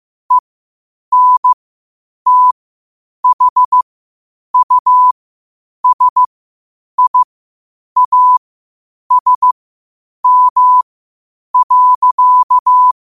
Morse code, heard as a keyed tone, about 15 wpm, spaced as for 8 wpm, 1000 Hz. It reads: ENTHUSIASM.
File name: qotdfiles / 8wpm